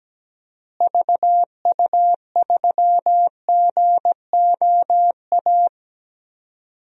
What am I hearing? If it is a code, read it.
Morse code, heard as a keyed tone, about 17 wpm, 700 Hz. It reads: VU3GOA
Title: DXCC-100 Call Signs - 6 Characters - Repeated 2x - 17wpm